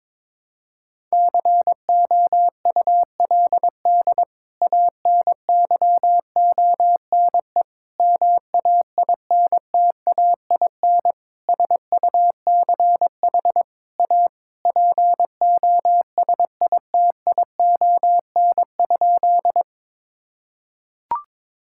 Morse code, heard as a keyed tone, about 22 wpm, 700 Hz. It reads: COULD ANYONE MAINTAIN SUCH A POSITION?
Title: Sentences from Top 1000 Words - Review Part 5 - 22wpm